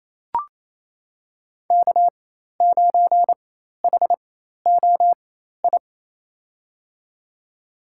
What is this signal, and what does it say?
Morse code, heard as a keyed tone, about 28 wpm, spaced as for 13 wpm, 700 Hz. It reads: K9HOI
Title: USA Call Signs - 5 Characters - 4x Character Spacing - 28wpm